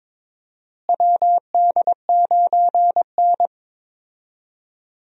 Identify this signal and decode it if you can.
Morse code, heard as a keyed tone, about 22 wpm, 700 Hz. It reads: WD9N
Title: USA Call Signs - 4 Characters - Repeated 3x - 22wpm